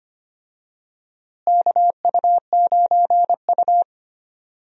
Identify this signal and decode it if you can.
Morse code, heard as a keyed tone, about 25 wpm, 700 Hz. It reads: KU9U
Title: USA Call Signs - 4 Characters - Repeated 3x - 25wpm